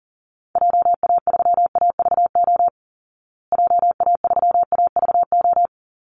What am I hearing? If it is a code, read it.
Morse code, heard as a keyed tone, about 40 wpm, 700 Hz. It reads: JA3AVO JA3AVO
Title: Call Signs Repeated 3x - 40wpm